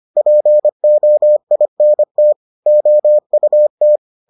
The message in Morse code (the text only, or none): POINT OUT